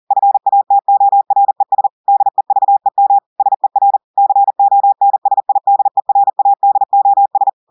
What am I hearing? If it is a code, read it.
Morse code, heard as a keyed tone, about 40 wpm, 800 Hz. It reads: FATORES DEVEM SER CONSIDERADOS
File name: audios / portugais / portugais texte en clair 2 vitesse 40 mots minutes